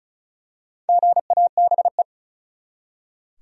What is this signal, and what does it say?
Morse code, heard as a keyed tone, about 36 wpm, 700 Hz. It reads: GABE
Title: USA Common Names - 35wpm